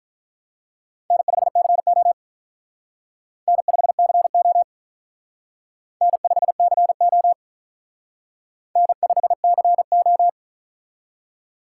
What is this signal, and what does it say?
Morse code, heard as a keyed tone, about 45 wpm, 700 Hz. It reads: N5CO N5CO N5CO N5CO